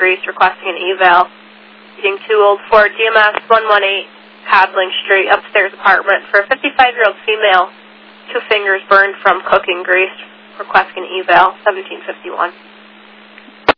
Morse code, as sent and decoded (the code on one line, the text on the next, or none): none